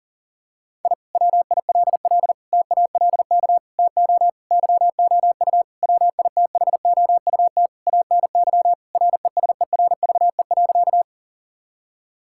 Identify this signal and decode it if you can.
Morse code, heard as a keyed tone, about 37 wpm, 700 Hz. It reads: I WILL TALK TO YOU WITHOUT ANY RESERVE.
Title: Sentences from Top 1000 Words - Review Part 6 - 40wpm